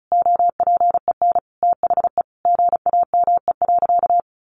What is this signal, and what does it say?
Morse code, heard as a keyed tone, about 34 wpm, 700 Hz. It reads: OPEN THE GAME.
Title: Sentences from Top 1500 Words 35wpm